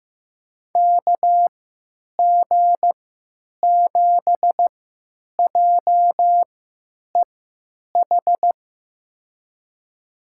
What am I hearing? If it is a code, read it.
Morse code, heard as a keyed tone, about 15 wpm, spaced as for 9 wpm, 700 Hz. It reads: KG7JEH
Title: USA Call Signs - 6 Characters - 3x Character Spacing - 15wpm